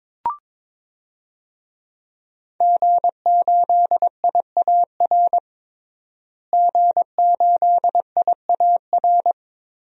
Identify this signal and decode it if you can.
Morse code, heard as a keyed tone, about 21 wpm, 700 Hz. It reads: G8IAR G8IAR
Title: DXCC-100 Call Signs - 5 Characters - Repeated 2x - 22wpm